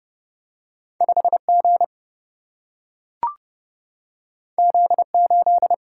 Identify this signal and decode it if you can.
Morse code, heard as a keyed tone, about 30 wpm, 700 Hz. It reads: 5 G Z 8
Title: Single-Letter Single-Number 30wpm